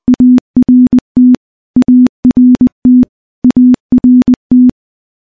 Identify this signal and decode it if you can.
Morse code, heard as a keyed tone, about 20 wpm, 250 Hz. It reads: ART ART ART
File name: morse / morse-training-20WPM